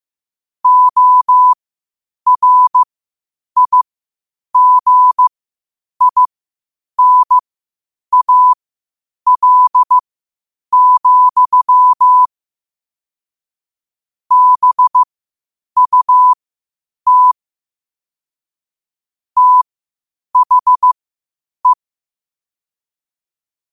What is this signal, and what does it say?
Morse code, heard as a keyed tone, about 15 wpm, spaced as for 8 wpm, 1000 Hz. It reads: ORIGINAL, BUT THE
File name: qotdfiles / 8wpm